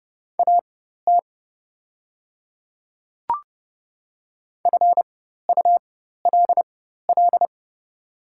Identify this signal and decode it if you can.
Morse code, heard as a keyed tone, about 29 wpm, spaced as for 14 wpm, 700 Hz. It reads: AT FULL
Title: 4x Character Spacing - Top 500 Words - 30wpm